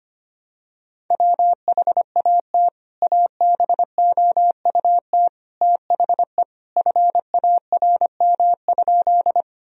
Morse code, heard as a keyed tone, about 25 wpm, 700 Hz. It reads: WHAT ABOUT THE FARM?
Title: Sentences from Top 1000 Words - Review Part 2 - 25wpm